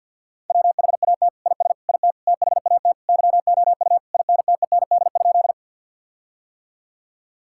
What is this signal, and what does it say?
Morse code, heard as a keyed tone, about 52 wpm, 700 Hz. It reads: WHAT IS IT THAT YOU INTEND?
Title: Sentences from Top 1600 Words 50wpm